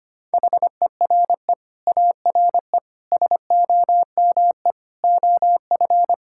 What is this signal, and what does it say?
Morse code, heard as a keyed tone, about 25 wpm, 700 Hz. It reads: HERE ARE SOME OF
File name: Sets of 9 Words - 25wpm